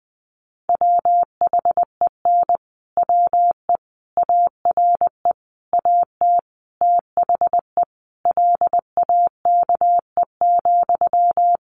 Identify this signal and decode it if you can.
Morse code, heard as a keyed tone, about 20 wpm, 700 Hz. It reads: WHEN WE ARE AT THE LAKE,
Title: Sentences from Top 1100 Words 20wpm